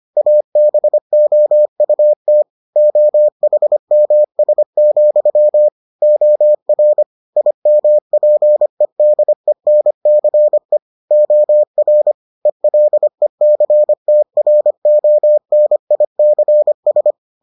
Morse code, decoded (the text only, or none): ABOUT OHMS, OR IMPEDENCE OR ELECTRONICS